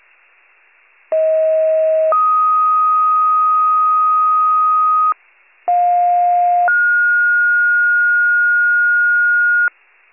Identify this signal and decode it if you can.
Two-tone page, heard as a one-second tone, then a three-second tone, about 640 then 1250 Hz, about 710 then 1430 Hz, with noise. It is heard from 1.1 to 9.7 s.
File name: pager / 46 FIRE / wcalert